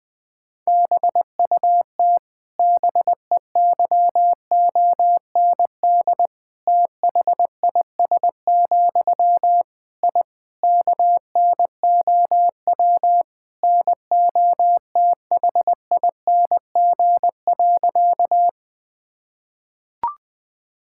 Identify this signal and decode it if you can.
Morse code, heard as a keyed tone, about 21 wpm, 700 Hz. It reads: BUT BEYOND THIS, I KNOW NOTHING.
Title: Sentences from Top 1000 Words - Review Part 6 - 20wpm